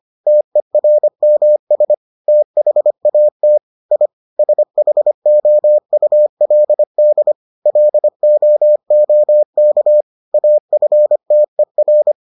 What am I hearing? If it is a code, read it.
Morse code, heard as a keyed tone, about 25 wpm, 600 Hz. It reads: TERMS THAT I SHOULD LOOK AFTER